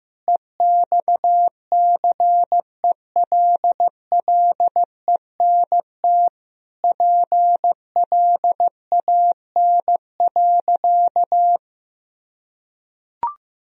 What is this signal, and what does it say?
Morse code, heard as a keyed tone, about 15 wpm, 700 Hz. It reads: EXCELLENT PLAN.
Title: Sentences from Top 1000 Words - Review Part 2 - 15wpm